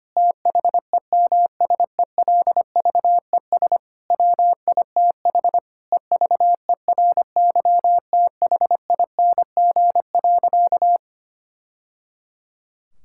Morse code, heard as a keyed tone, about 25 wpm, 700 Hz. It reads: THEMSELVES WITH EVERYTHING.
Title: Sentences from Top 1000 Words - Review Part 5 - 25wpm